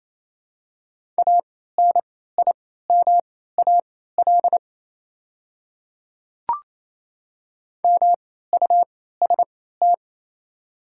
Morse code, heard as a keyed tone, about 28 wpm, spaced as for 16 wpm, 690 Hz. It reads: ANIMAL MUST
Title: 3x Character Spacing - Top 200 Words - 28wpm